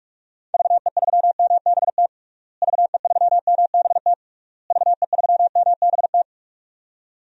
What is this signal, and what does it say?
Morse code, heard as a keyed tone, about 46 wpm, 700 Hz. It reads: VE3MBT VE3MBT VE3MBT
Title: DXCC-100 Call Signs - 6 Characters - Repeated 3x - 45wpm